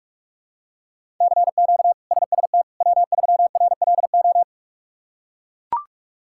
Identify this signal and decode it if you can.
Morse code, heard as a keyed tone, about 43 wpm, 700 Hz. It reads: CQ SST W3RLO
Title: K1USN SST QSO Practice - 45wpm